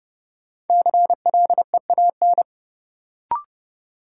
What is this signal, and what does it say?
Morse code, heard as a keyed tone, about 30 wpm, 700 Hz. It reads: CLEAN
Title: Sentences from Top 1100 Words - New Word Drill - 30wpm